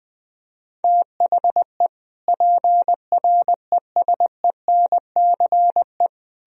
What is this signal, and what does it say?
Morse code, heard as a keyed tone, about 22 wpm, 700 Hz. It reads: THE PRESENCE
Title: Speed-Racing - Sets of 3 Words 20wpm